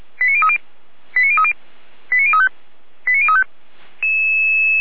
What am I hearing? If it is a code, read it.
Five-tone selective call (ZVEI), heard as quick runs of five tones, tones of about 80 ms. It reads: 89030 89030 89035 89035
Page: SDR/ZVEI